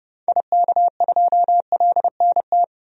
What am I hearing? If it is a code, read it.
Morse code, heard as a keyed tone, about 30 wpm, 700 Hz. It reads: IK2LNT
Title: DXCC-100 Call Signs - 6 Characters - 30wpm